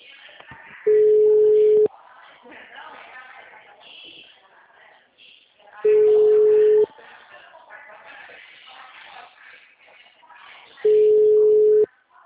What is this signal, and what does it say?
Ringback tone (Europe, 425 Hz), with speech.
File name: out- -2048- - - .81558